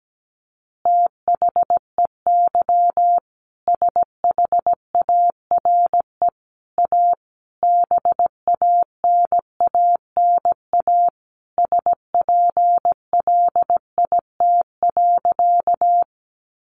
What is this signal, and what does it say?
Morse code, heard as a keyed tone, about 17 wpm, 700 Hz. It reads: THEY SHARE A BANANA SPLIT.
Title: Additional Easy Phrases 17wpm